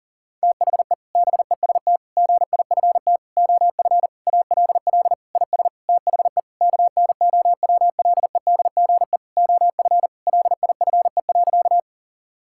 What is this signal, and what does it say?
Morse code, heard as a keyed tone, about 40 wpm, 700 Hz. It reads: THE BEST GIFT OF ALL IS THE KNOWLEDGE OF LIFE.
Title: Sentences from Top 1000 Words - Review Part 6 - 40wpm